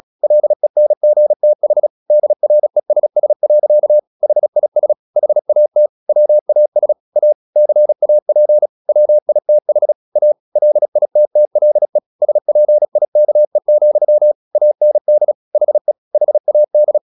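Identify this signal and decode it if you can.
Morse code, heard as a keyed tone, about 36 wpm, 600 Hz. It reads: LENGTH DRESS. HIS HAT WAS A CAP WITH A LITTLE SPIKE, AND HE HAD
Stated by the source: Morse practice at 36 wpm, text of Ken Follett's novel Never, from CW med Gnister.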